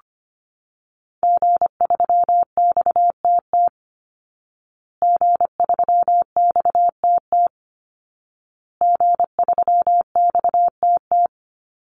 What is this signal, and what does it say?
Morse code, heard as a keyed tone, about 25 wpm, 700 Hz. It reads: G3XTT G3XTT G3XTT
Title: Call Signs Repeated 2x - 25wpm